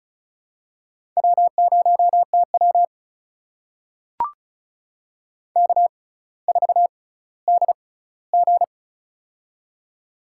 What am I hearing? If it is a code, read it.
Morse code, heard as a keyed tone, about 36 wpm, spaced as for 18 wpm, 700 Hz. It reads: W0TW K4DG
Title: USA Call Signs - 6x Character Spacing - 35wpm